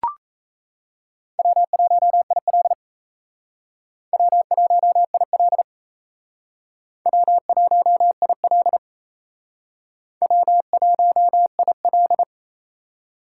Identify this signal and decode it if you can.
Morse code, heard as a keyed tone, about 34 wpm, 700 Hz. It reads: W1IL W1IL W1IL W1IL
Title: Speed Racing - USA 1x2 Call Signs 28wpm